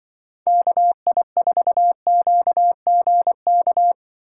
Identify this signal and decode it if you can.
Morse code, heard as a keyed tone, about 24 wpm, 700 Hz. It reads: KI4QGK
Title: Speed Racing - USA 2x3 Call Signs 20wpm